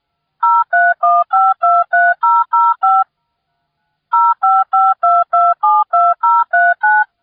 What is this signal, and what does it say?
DTMF, heard as telephone touch tones, tones of about 220 ms. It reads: 031523005 0552272039